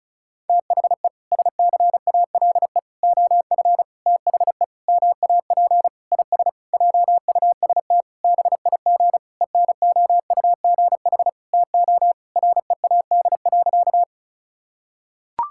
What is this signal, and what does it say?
Morse code, heard as a keyed tone, about 36 wpm, 700 Hz. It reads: THE SCALE OF THE MAP IS JUST BIG ENOUGH TO READ.
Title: Sentences from Top 1000 Words - Review Part 5 - 35wpm